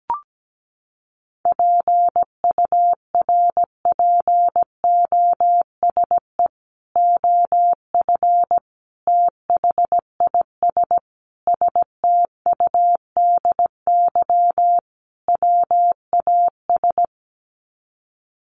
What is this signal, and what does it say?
Morse code, heard as a keyed tone, about 17 wpm, 700 Hz. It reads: PURPOSE OF THIS STUDY WAS
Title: Sets of 5 Words 17wpm